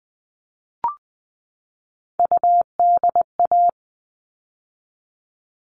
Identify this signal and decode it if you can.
Morse code, heard as a keyed tone, about 20 wpm, 700 Hz. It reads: UDA
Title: All Cut Numbers - 3-Digit 20wpm